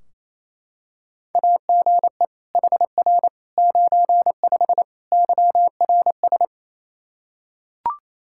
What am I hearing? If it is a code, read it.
Morse code, heard as a keyed tone, about 27 wpm, 700 Hz. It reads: AGE HR 95 YRS
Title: Practice QSO - Exchange 3 - More Info - 28wpm